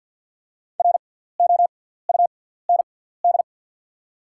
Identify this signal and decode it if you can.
Morse code, heard as a keyed tone, about 52 wpm, spaced as for 17 wpm, 700 Hz. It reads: ROUND
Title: 6x Character Spacing - Top 300 Words - 50wpm